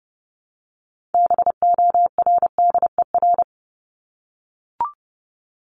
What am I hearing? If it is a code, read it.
Morse code, heard as a keyed tone, about 30 wpm, 700 Hz. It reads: BORDER
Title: Top 1751-2000 Words 30wpm